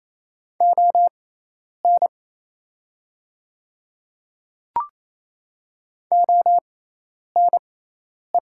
Morse code, heard as a keyed tone, about 27 wpm, spaced as for 10 wpm, 700 Hz. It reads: ON ONE
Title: 6x Character Spacing - Top 100 Words - 28wpm